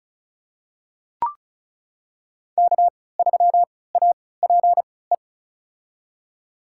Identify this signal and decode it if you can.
Morse code, heard as a keyed tone, about 36 wpm, spaced as for 20 wpm, 700 Hz. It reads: K3APE
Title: USA Call Signs - 5 Characters - 3x Character Spacing - 35wpm